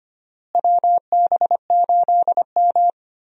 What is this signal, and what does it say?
Morse code, heard as a keyed tone, about 25 wpm, 700 Hz. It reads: WB8M
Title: USA Call Signs - 4 Characters - Repeated 3x - 25wpm